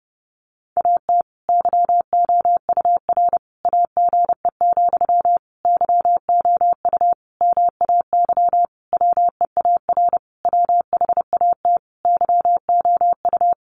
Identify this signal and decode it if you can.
Morse code, heard as a keyed tone, about 30 wpm, 700 Hz. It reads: AT YOUR AGE, YOU MAY WEAR WHAT YOU
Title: Sentences from Top 1500 Words 30wpm